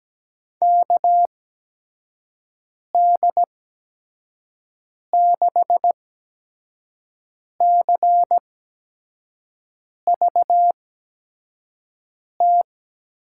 Morse code, heard as a keyed tone, about 17 wpm, spaced as for 5 wpm, 700 Hz. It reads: KD6CVT
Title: USA Call Signs - 8x Character Spacing - 17wpm